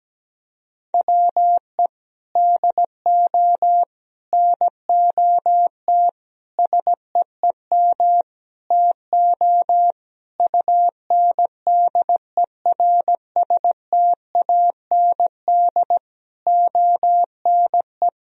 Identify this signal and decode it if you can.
Morse code, heard as a keyed tone, about 17 wpm, 700 Hz. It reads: WE DO NOT SEEM TO UNDERSTAND ONE
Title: Sentences from Top 1000 Words - Review Part 1 - 17wpm